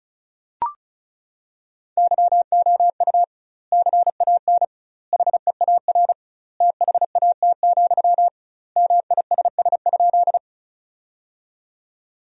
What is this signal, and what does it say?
Morse code, heard as a keyed tone, about 34 wpm, 700 Hz. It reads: YOU CAN HEAR THAT, MISS?
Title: 2x Word-Spacing - Sentences from Top 500 Words - 35wpm